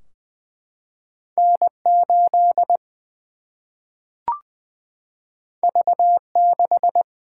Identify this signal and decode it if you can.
Morse code, heard as a keyed tone, about 20 wpm, 700 Hz. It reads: N 8 V 6
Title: Single-Letter Single-Number 20wpm